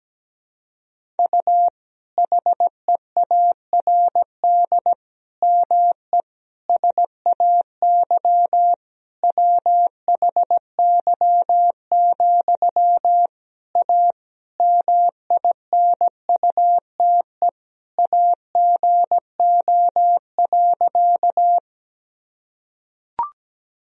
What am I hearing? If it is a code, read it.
Morse code, heard as a keyed tone, about 17 wpm, 700 Hz. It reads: U HEARD ME SAY WHY, A MINUTE AGO.
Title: Sentences from Top 500 Words and Abbreviations 17wpm